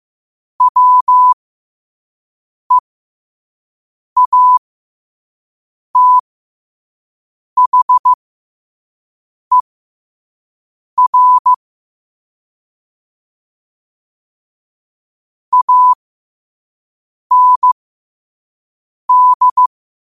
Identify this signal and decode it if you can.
Morse code, heard as a keyed tone, about 15 wpm, spaced as for 5 wpm, 1000 Hz. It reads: WEATHER AND
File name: qotdfiles / 5wpm